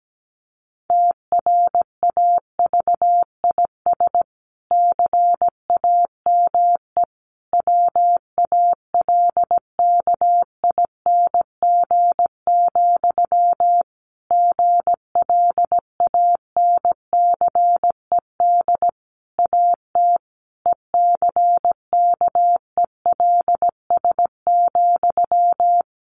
Morse code, heard as a keyed tone, about 17 wpm, 700 Hz. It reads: TRAVIS CAME WALKING, GLANCED AT ECKELS,
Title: A Sound Of Thunder 17wpm